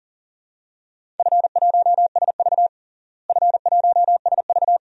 Morse code, heard as a keyed tone, about 37 wpm, 700 Hz. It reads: F1SV F1SV
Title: DXCC-100 Call Signs - 4 Characters - Repeated 2x - 40wpm